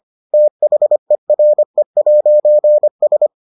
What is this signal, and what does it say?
Morse code, heard as a keyed tone, about 25 wpm, 600 Hz. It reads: THERE'S